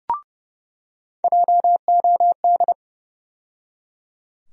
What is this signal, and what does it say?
Morse code, heard as a keyed tone, about 30 wpm, 700 Hz. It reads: JOD
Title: USA Call Sign Postfixes - 3 Letters 30wpm